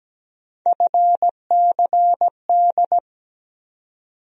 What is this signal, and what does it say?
Morse code, heard as a keyed tone, about 17 wpm, 700 Hz. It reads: FCD
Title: USA Call Sign Postfixes - 3 Letters 17wpm